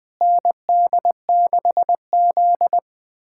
Morse code, heard as a keyed tone, about 21 wpm, 700 Hz. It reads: ND6Z
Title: USA Callsign DE Callsign - 4 Characters - 20wpm